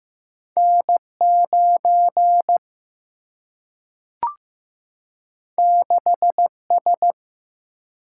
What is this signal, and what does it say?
Morse code, heard as a keyed tone, about 15 wpm, 700 Hz. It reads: N 9 6 S
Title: Single-Letter Single-Number 15wpm